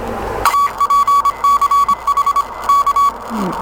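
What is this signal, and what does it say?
Morse code, heard as a keyed tone, about 25 wpm, 1130 Hz. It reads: APCH K
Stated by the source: Signal of UA1CEG/p, a portable 5-watt QRP transceiver, in the Kulikovo Polye Contest, as received by PA2PCH.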